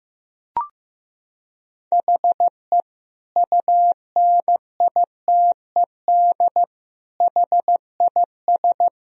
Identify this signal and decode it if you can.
Morse code, heard as a keyed tone, about 15 wpm, 700 Hz. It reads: HE UNITED HIS
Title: Sentences from Top 1000 Words - Review Part 4 - 15wpm